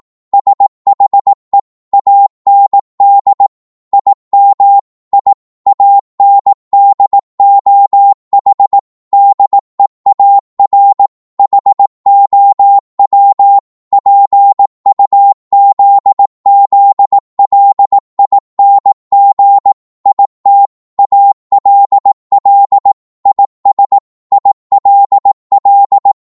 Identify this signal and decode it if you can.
Morse code, heard as a keyed tone, about 18 wpm, 800 Hz. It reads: SHE AND IM I AND OH DEAR HOW PUZZLING IT ALL IS ILL